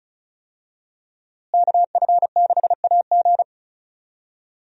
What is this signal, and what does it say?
Morse code, heard as a keyed tone, about 34 wpm, 700 Hz. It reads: KF6AG
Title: USA Call Signs - 5 Characters - Repeated 2x - 35wpm